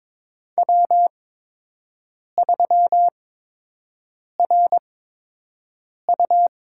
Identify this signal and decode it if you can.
Morse code, heard as a keyed tone, about 21 wpm, spaced as for 6 wpm, 700 Hz. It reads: W3RU
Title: USA Call Signs - 8x Character Spacing - 22wpm